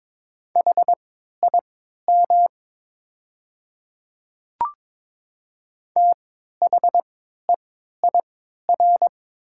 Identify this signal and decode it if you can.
Morse code, heard as a keyed tone, about 22 wpm, spaced as for 12 wpm, 700 Hz. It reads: HIM THEIR
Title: 3x Character Spacing - Top 100 Words - 22wpm